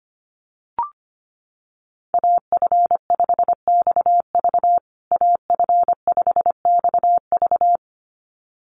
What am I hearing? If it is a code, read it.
Morse code, heard as a keyed tone, about 25 wpm, 700 Hz. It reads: AF5XV AF5XV
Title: USA Call Signs Repeated 2x 25wpm